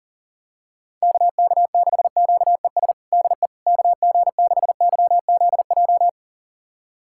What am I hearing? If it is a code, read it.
Morse code, heard as a keyed tone, about 37 wpm, 700 Hz. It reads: KK6QES DE KG6YZJ
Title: USA Callsign DE Callsign - 6 Characters - 40wpm